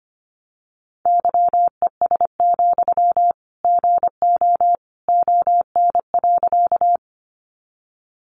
Morse code, heard as a keyed tone, about 25 wpm, 700 Hz. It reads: YES, GO ON.
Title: Sentences from Top 500 Words 25wpm